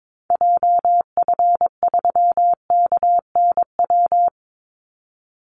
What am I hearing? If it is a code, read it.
Morse code, heard as a keyed tone, about 22 wpm, 700 Hz. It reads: JF3KNW
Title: Call Signs Repeated 3x - 22wpm